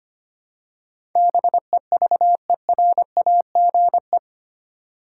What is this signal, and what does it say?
Morse code, heard as a keyed tone, about 25 wpm, 700 Hz. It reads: BEVERAGE
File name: Eight Letter Words - 25wpm